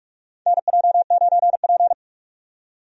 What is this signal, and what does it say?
Morse code, heard as a keyed tone, about 46 wpm, 700 Hz. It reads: NJ9P
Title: USA Call Signs - 4 Characters - 45wpm